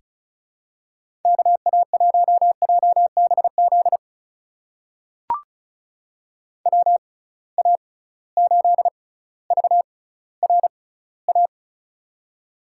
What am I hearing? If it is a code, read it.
Morse code, heard as a keyed tone, about 36 wpm, spaced as for 18 wpm, 700 Hz. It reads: KA1JBZ WA8VRA
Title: USA Call Signs - 6 Characters - 6x Character Spacing - 35wpm